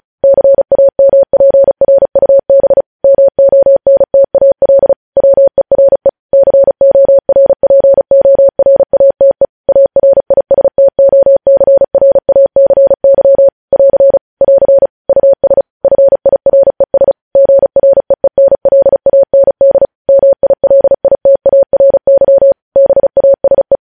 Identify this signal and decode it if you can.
Morse code, heard as a keyed tone, about 36 wpm, 550 Hz. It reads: CAMPRUB MONTAL WERE CORPORATE ARISTOCRACY <AR> <AR> US FIRES GREENLAND MILITARY BASE
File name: podcast / News headlines-1x-35x35